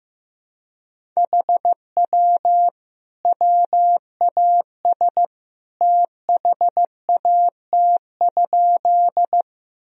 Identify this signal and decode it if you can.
Morse code, heard as a keyed tone, about 15 wpm, 700 Hz. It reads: HW WAS THAT?
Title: Sentences from Top 100 Words and Abbreviations 15wpm